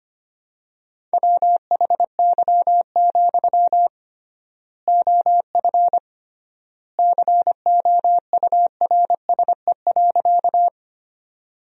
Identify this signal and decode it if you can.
Morse code, heard as a keyed tone, about 25 wpm, 700 Hz. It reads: WHY, OF COURSE.
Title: 3x Word-Spacing - Sentences from Top 500 Words - 25wpm